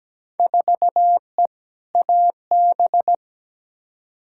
Morse code, heard as a keyed tone, about 17 wpm, 700 Hz. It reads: 4E AB
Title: Field Day Exchanges Encore - 17wpm